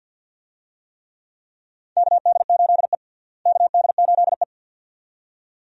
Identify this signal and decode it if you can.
Morse code, heard as a keyed tone, about 52 wpm, 700 Hz. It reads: KD8E KD8E
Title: USA Call Signs - 4 Characters - Repeated 2x - 50wpm